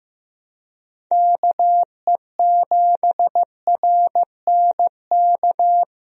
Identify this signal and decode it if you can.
Morse code, heard as a keyed tone, about 15 wpm, 700 Hz. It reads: KE7RNK
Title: USA Call Signs - 6 Characters - Repeated 2x - 15wpm